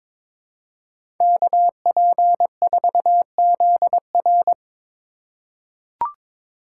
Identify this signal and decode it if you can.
Morse code, heard as a keyed tone, about 22 wpm, 690 Hz. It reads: KP4ZR
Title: USA Call Signs - 5 Characters - 2x Character Spacing - 22wpm